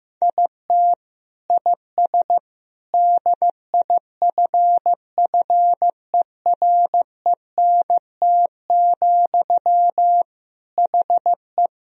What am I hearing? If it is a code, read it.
Morse code, heard as a keyed tone, about 16 wpm, 700 Hz. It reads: IT IS DIFFERENT, HE
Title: Sentences from Top 1000 Words - Review Part 3 - 15wpm